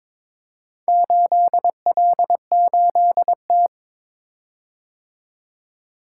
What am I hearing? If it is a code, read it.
Morse code, heard as a keyed tone, about 22 wpm, 700 Hz. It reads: 8L8T
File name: practice / 22wpm / Four Letter-Number Combinations - 22wpm